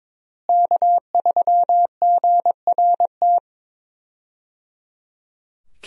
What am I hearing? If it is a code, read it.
Morse code, heard as a keyed tone, about 22 wpm, 700 Hz. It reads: K3GRT
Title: USA Call Signs - 5 Characters - Repeated 3x - 22wpm